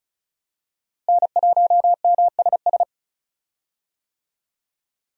Morse code, heard as a keyed tone, about 34 wpm, 700 Hz. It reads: N1MSS
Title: USA Call Signs - 5 Characters - Repeated 2x - 35wpm